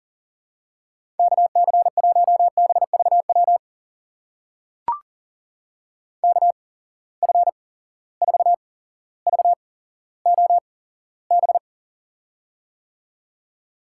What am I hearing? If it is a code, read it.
Morse code, heard as a keyed tone, about 40 wpm, spaced as for 17 wpm, 700 Hz. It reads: KC1BVW KF4VOB
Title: USA Call Signs - 6 Characters - 8x Character Spacing - 40wpm